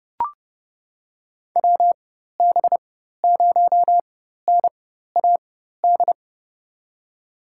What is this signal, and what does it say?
Morse code, heard as a keyed tone, about 30 wpm, spaced as for 14 wpm, 700 Hz. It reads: WB0NAD
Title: USA Call Signs - 4x Character Spacing - 30wpm